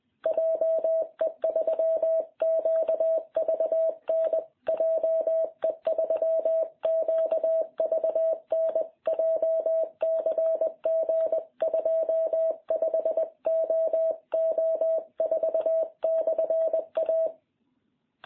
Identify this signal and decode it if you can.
Morse code, heard as a keyed tone, about 20 wpm, 660 Hz. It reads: JE3QVN JE3QVN JCG 25OO4/A